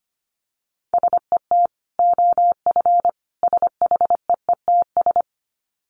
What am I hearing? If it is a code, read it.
Morse code, heard as a keyed tone, about 25 wpm, 700 Hz. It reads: SET OF SHEETS
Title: Dit Challenge 25wpm